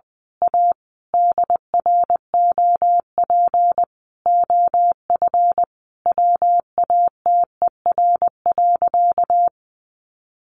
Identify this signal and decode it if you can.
Morse code, heard as a keyed tone, about 20 wpm, 700 Hz. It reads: A DROP OF WATER.
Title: Sentences from Top 1400 Words 20wpm